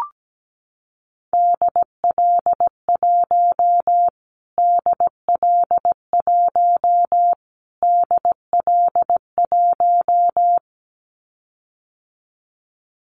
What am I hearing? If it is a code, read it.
Morse code, heard as a keyed tone, about 17 wpm, 700 Hz. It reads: DL1 DL1 DL1
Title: Callsign Prefixes for Top 25 DXCC - Repeated 3x - 17wpm